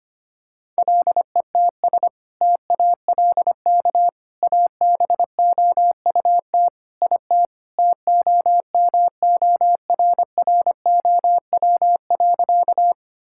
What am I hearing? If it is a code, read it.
Morse code, heard as a keyed tone, about 25 wpm, 700 Hz. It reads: LETS TALK ABOUT IT TOMORROW.
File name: practice / 25wpm / Sentences with 2-Word Phrases - 25wpm